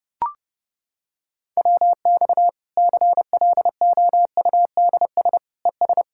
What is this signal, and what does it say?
Morse code, heard as a keyed tone, about 32 wpm, 700 Hz. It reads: WX CLOUDS ES
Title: Practice QSO - Exchange 2 - Info - 30wpm